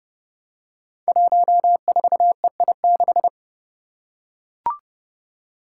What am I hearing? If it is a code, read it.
Morse code, heard as a keyed tone, about 30 wpm, 700 Hz. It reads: 14EI6
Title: Five Challenging Letter-Number Combinations 30wpm